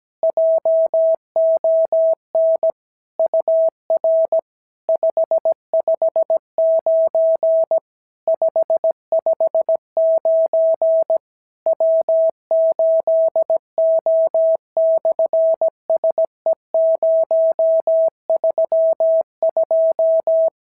Morse code, heard as a keyed tone, about 17 wpm, 650 Hz. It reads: JON UR 559 559 W8O/SE032
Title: SOTA QSO Practice 17wpm